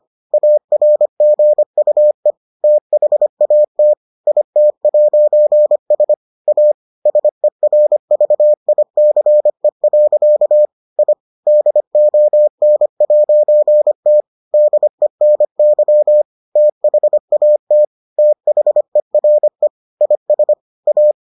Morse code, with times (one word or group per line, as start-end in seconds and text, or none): 0.3-2.3 ARGUE
2.6-3.9 THAT
4.3-6.1 IT'S
6.5-6.7 A
7.1-10.7 SERVICE.
11.0-11.1 I
11.5-14.2 DON'T
14.5-16.2 DENY
16.6-17.8 THAT
18.2-19.7 THERE
20.0-20.5 IS
20.9-21.1 A